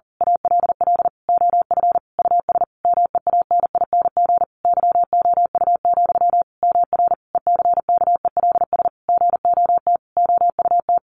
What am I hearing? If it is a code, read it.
Morse code, heard as a keyed tone, about 40 wpm, 700 Hz. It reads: ALL OF US MEANING YOU, MR ECKELS GOT OUT
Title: A Sound Of Thunder 40wpm